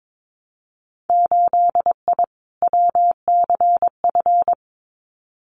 Morse code, heard as a keyed tone, about 22 wpm, 700 Hz. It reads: 8I WCF
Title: Winter Field Day Exchanges Encore 22wpm